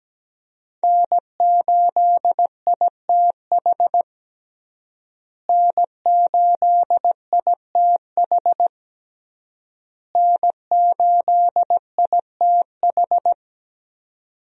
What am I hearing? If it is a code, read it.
Morse code, heard as a keyed tone, about 17 wpm, 700 Hz. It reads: N8ITH N8ITH N8ITH
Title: USA Call Signs - 5 Characters - Repeated 3x - 17wpm